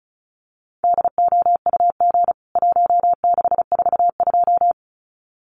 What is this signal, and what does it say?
Morse code, heard as a keyed tone, about 34 wpm, 700 Hz. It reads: DOUG 1642
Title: CWT Exchange 35wpm